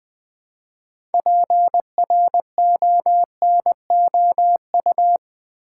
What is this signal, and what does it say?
Morse code, heard as a keyed tone, about 20 wpm, 700 Hz. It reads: PRONOU
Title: Build A Word - 3 Letters At A Time - 9-Letter Words - 20wpm